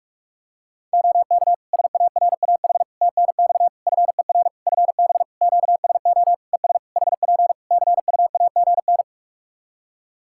Morse code, heard as a keyed tone, about 41 wpm, 700 Hz. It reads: OK SARAH TNX FER FB QSO ES HP CUAGN
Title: Practice QSO - Exchange 4 - Ending - 45wpm